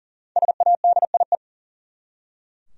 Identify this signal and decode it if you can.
Morse code, heard as a keyed tone, about 40 wpm, 700 Hz. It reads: SADIE
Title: USA Common Names - Max 5 Letters - 40wpm